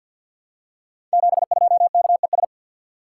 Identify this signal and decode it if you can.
Morse code, heard as a keyed tone, about 47 wpm, 700 Hz. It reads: 72KES
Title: Code Groups - 5 Letter:Numbers - 50wpm